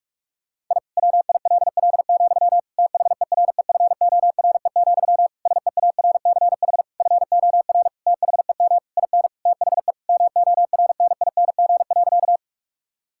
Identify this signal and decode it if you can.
Morse code, heard as a keyed tone, about 46 wpm, 700 Hz. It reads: I WILL, THEREFORE, SEARCH FOR THEM IN THE MORNING.
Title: Sentences from Top 1000 Words - Review Part 5 - 45wpm